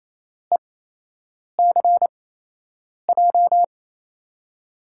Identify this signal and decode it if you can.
Morse code, heard as a keyed tone, about 28 wpm, spaced as for 8 wpm, 700 Hz. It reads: ECJ
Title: USA Call Signs - 6 Characters - 8x Character Spacing - 28wpm